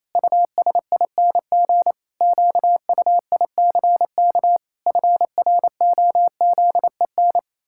Morse code, heard as a keyed tone, about 28 wpm, 700 Hz. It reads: USING QUICK FROZEN
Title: Pangram Challenge v2 - 28wpm